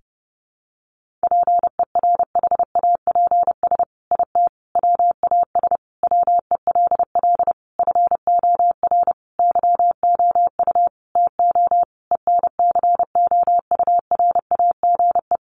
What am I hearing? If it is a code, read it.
Morse code, heard as a keyed tone, about 30 wpm, 700 Hz. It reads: PERHAPS IT WAS WELL FOR YOU TO ENCOURAGE
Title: Sentences from Top 1400 Words 30wpm